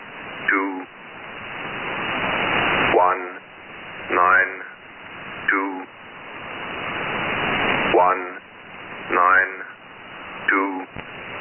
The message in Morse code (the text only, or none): none